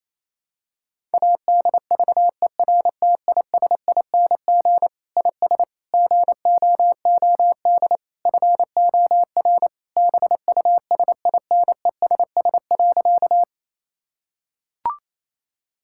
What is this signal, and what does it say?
Morse code, heard as a keyed tone, about 28 wpm, 700 Hz. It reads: ADVERTISING IS GOOD FOR BUSINESS.
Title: Sentences from Top 1000 Words - Review Part 6 - 28wpm